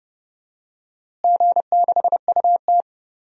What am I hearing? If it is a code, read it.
Morse code, heard as a keyed tone, about 31 wpm, 700 Hz. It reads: G6UT
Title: DXCC-100 Call Signs - 4 Characters - Repeated 2x - 30wpm